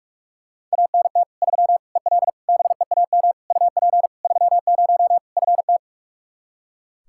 Morse code, heard as a keyed tone, about 43 wpm, 700 Hz. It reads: ANT 3 EL BEAM UP 30 FT
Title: Practice QSO - Exchange 2 - Info - 45wpm